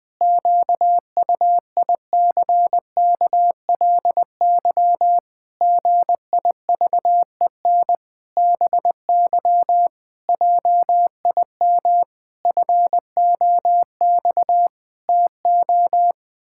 Morse code, heard as a keyed tone, about 21 wpm, 700 Hz. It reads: QUICKLY GIVEN BY JIM FOX TO
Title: Pangram Challenge v2 - 20wpm